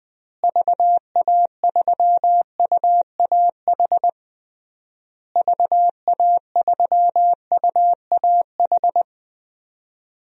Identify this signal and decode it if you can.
Morse code, heard as a keyed tone, about 21 wpm, 700 Hz. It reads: VA3UAH VA3UAH
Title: DXCC-100 Call Signs - 6 Characters - Repeated 3x - 20wpm